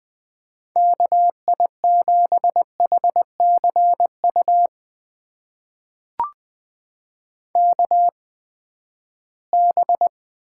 Morse code, heard as a keyed tone, about 20 wpm, spaced as for 14 wpm, 700 Hz. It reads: KI7HCU KB
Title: USA Call Signs - 6 Characters - 8x Character Spacing - 20wpm